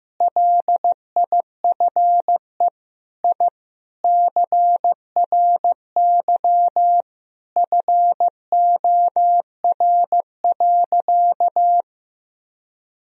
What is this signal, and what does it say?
Morse code, heard as a keyed tone, about 15 wpm, 700 Hz. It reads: LIFE I CRY FOR.
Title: Sentences from Top 1000 Words - Review Part 2 - 15wpm